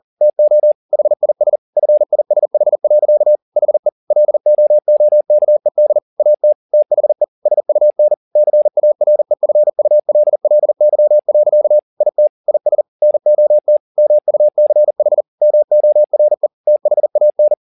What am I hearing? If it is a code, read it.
Morse code, heard as a keyed tone, about 40 wpm, 600 Hz. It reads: TO HIS FISH. HE LOOKED AT THE SUN CAREFULLY. IT IS NOT MUCH MORE THAN
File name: the old man and the sea 2 40wpm